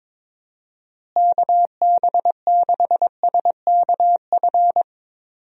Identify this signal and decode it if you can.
Morse code, heard as a keyed tone, about 22 wpm, 700 Hz. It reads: KB6SKF
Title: USA Call Signs - 6 Characters - Repeated 2x - 22wpm